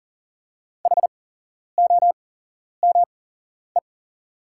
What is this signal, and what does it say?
Morse code, heard as a keyed tone, about 40 wpm, spaced as for 11 wpm, 690 Hz. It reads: HOME
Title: 8x Character Spacing - Top 500 Words - 40wpm